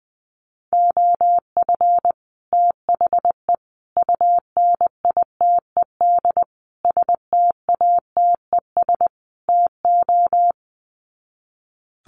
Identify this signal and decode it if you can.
Morse code, heard as a keyed tone, about 20 wpm, 700 Hz. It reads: OF THE UNITED STATES TO
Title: Sets of 5 Words Encore 20wpm